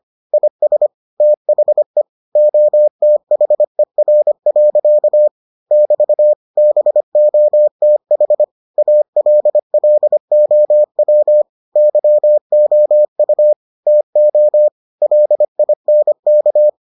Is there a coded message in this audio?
Yes: Morse code from 0.3 to 16.7 s, heard as a keyed tone, about 25 wpm, 600 Hz. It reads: IS THE OTHER. <BT> BOTH ALLOW YOU TO LINK